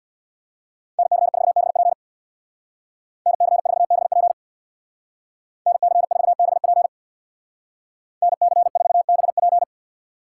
Morse code, heard as a keyed tone, about 65 wpm, 700 Hz. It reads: NC4BP NC4BP NC4BP NC4BP